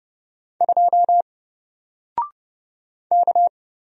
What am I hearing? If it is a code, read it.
Morse code, heard as a keyed tone, about 31 wpm, 700 Hz. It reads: 2 K
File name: Single-Letter-Number-30wpm